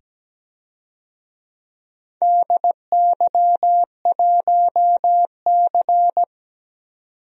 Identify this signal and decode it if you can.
Morse code, heard as a keyed tone, about 17 wpm, 700 Hz. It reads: DY1C
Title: DXCC-100 Call Signs - 4 Characters - 17wpm